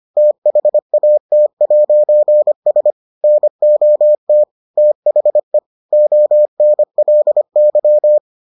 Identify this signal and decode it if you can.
Morse code, heard as a keyed tone, about 25 wpm, 600 Hz. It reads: THAT'S NOT THE ONLY